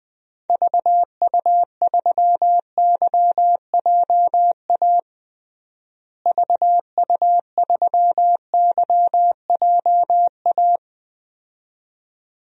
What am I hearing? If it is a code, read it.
Morse code, heard as a keyed tone, about 21 wpm, 690 Hz. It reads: VU3YJA VU3YJA
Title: DXCC-100 Call Signs - 6 Characters - Repeated 2x - 20wpm